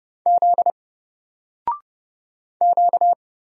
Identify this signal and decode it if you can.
Morse code, heard as a keyed tone, about 30 wpm, 700 Hz. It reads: Z Q
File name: Single-Letter-Number-30wpm